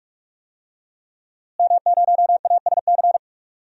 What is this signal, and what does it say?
Morse code, heard as a keyed tone, about 41 wpm, 700 Hz. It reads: M0ASC
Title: DXCC-100 Call Signs - 5 Characters - Repeated 2x - 45wpm